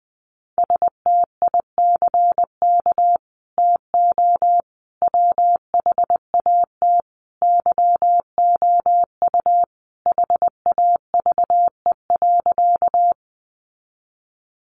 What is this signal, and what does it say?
Morse code, heard as a keyed tone, about 19 wpm, 700 Hz. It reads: STICK TO WHAT YOU HAVE.
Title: Sentences from Top 1200 Words 20wpm